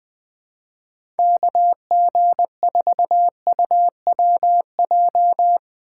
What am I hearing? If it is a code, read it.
Morse code, heard as a keyed tone, about 20 wpm, 700 Hz. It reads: KG4UWJ
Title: USA Call Signs - 6 Characters - 2x Character Spacing - 20wpm